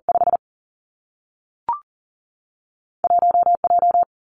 Morse code, heard as a keyed tone, about 39 wpm, 700 Hz. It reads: 5 1 J
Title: Two Challenging Letter-Number Combinations 40wpm